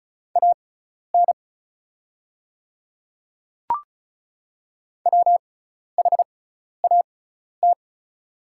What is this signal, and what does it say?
Morse code, heard as a keyed tone, about 36 wpm, spaced as for 12 wpm, 690 Hz. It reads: AN WHAT
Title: 6x Character Spacing - Top 100 Words - 35wpm